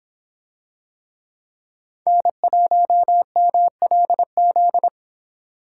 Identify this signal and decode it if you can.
Morse code, heard as a keyed tone, about 23 wpm, 700 Hz. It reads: N1MLZ